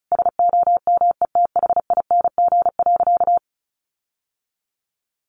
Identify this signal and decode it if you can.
Morse code, heard as a keyed tone, about 34 wpm, 700 Hz. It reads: SOMETHING.
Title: Sentences from Top 1400 Words 35wpm